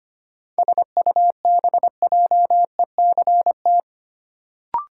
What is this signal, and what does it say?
Morse code, heard as a keyed tone, about 25 wpm, 700 Hz. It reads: SUBJECT
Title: Sentences from Top 800 Words - New Word Drill - 25wpm